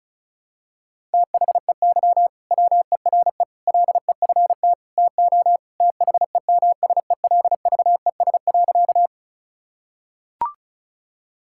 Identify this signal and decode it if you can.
Morse code, heard as a keyed tone, about 36 wpm, 700 Hz. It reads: THEY WERE LEFT TO THEMSELVES.
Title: Sentences from Top 1000 Words - Review Part 5 - 35wpm